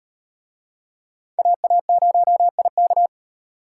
Morse code, heard as a keyed tone, about 38 wpm, 700 Hz. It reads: AA0IK